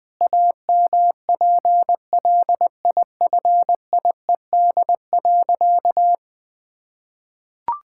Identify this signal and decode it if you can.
Morse code, heard as a keyed tone, about 20 wpm, 700 Hz. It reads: AMPLIFIED.
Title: Pangram Challenge v2 - 20wpm